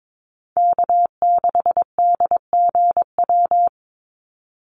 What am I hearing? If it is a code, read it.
Morse code, heard as a keyed tone, about 22 wpm, 700 Hz. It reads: K6DGW
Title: Call Signs Repeated 3x - 22wpm